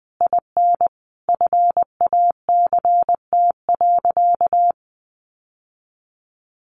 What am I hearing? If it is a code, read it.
Morse code, heard as a keyed tone, about 20 wpm, 700 Hz. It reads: IN FACT.
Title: Sentences from Top 600 Words 20wpm